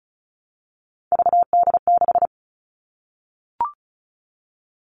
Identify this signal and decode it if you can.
Morse code, heard as a keyed tone, about 36 wpm, 700 Hz. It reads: VD6
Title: Three Challenging Letter-Number Combinations 35wpm